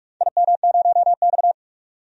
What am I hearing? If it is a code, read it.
Morse code, heard as a keyed tone, about 42 wpm, 700 Hz. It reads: IM0X
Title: DXCC-100 Call Signs - 4 Characters - 45wpm